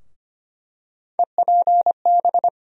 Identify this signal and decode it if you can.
Morse code, heard as a keyed tone, about 25 wpm, 700 Hz. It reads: EPB
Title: USA Call Sign Postfixes - 3 Letters 25wpm